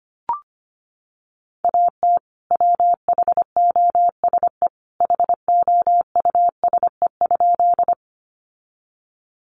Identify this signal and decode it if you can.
Morse code, heard as a keyed tone, about 25 wpm, 700 Hz. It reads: AT WHOSE HOUSE?
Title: Sentences from Top 1200 Words 25wpm